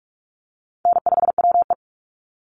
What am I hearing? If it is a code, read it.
Morse code, heard as a keyed tone, about 44 wpm, 700 Hz. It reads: N5WE
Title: Call Signs Repeated 2x - 45wpm